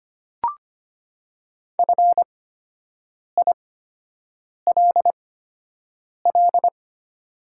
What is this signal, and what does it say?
Morse code, heard as a keyed tone, about 25 wpm, spaced as for 7 wpm, 700 Hz. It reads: FILL
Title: 8x Character Spacing - Top 500 Words - 25wpm